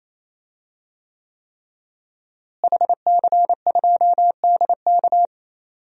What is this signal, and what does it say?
Morse code, heard as a keyed tone, about 28 wpm, 700 Hz. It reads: HC2DK
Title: DXCC-100 Call Signs - 5 Characters - Repeated 3x - 28wpm